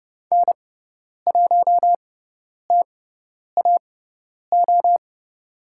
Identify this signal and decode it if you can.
Morse code, heard as a keyed tone, about 30 wpm, spaced as for 10 wpm, 700 Hz. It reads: N1TAO
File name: Lesson 16g - Letter c - Call Signs 30wpm